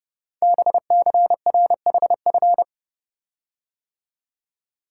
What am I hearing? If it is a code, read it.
Morse code, heard as a keyed tone, about 30 wpm, 700 Hz. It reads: BCRHF
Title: Code Groups - 5 Letters - 30wpm